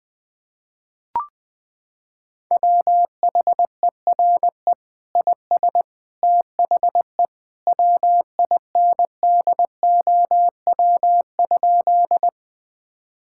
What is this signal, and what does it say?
Morse code, heard as a keyed tone, about 20 wpm, 700 Hz. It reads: WHERE IS THE WINDOW?
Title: Sentences from Top 1600 Words 20wpm